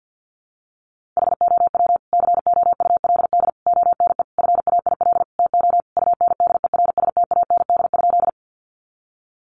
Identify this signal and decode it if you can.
Morse code, heard as a keyed tone, about 50 wpm, 700 Hz. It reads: HOW COULD ONE FAIL TO UNDERSTAND?
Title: Sentences from Top 1100 Words 50wpm